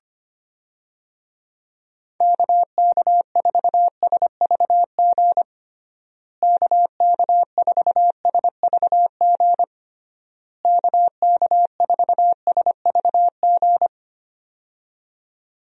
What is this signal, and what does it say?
Morse code, heard as a keyed tone, about 25 wpm, 700 Hz. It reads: KK4SVG KK4SVG KK4SVG
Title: USA Call Signs - 6 Characters - Repeated 3x - 25wpm